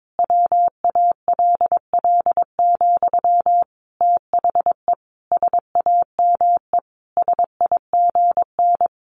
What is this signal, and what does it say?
Morse code, heard as a keyed tone, about 22 wpm, 700 Hz. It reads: WALL, THE SAME SIGN
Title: A Sound Of Thunder 22wpm